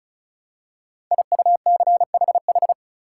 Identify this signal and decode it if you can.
Morse code, heard as a keyed tone, about 36 wpm, 700 Hz. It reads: IUCHH
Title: Code Groups - 5 Letters - 35wpm